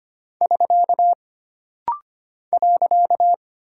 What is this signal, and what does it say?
Morse code, heard as a keyed tone, about 25 wpm, 700 Hz. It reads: <SK> .
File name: Single Punctuation-Prosign - Rapid-Fire - 25wpm